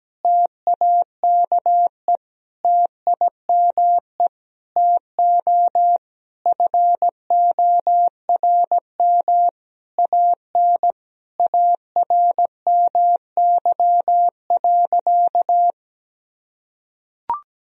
Morse code, heard as a keyed tone, about 17 wpm, 700 Hz. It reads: TAKE TIME TO FORM AN ARMY.
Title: Sentences from Top 1000 Words - Review Part 1 - 17wpm